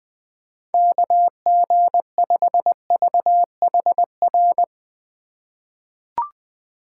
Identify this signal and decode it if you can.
Morse code, heard as a keyed tone, about 20 wpm, 700 Hz. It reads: KG5VHR
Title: USA Call Signs - 6 Characters - 3x Character Spacing - 20wpm